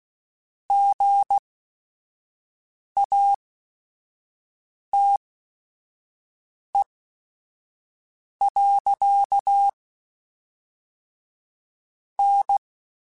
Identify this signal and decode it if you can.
Morse code, heard as a keyed tone, about 16 wpm, spaced as for 5 wpm, 800 Hz. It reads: GATE. N